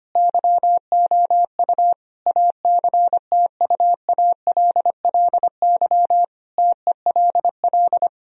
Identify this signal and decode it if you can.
Morse code, heard as a keyed tone, about 25 wpm, 700 Hz. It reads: YOU ACTUALLY TELL
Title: Sentences from Top 1000 Words - Review Part 4 - 25wpm